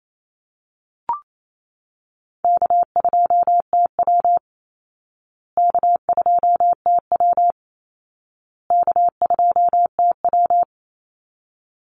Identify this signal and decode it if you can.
Morse code, heard as a keyed tone, about 28 wpm, 700 Hz. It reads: K2TW K2TW K2TW
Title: Easy Call Signs Repeated 3x - 28wpm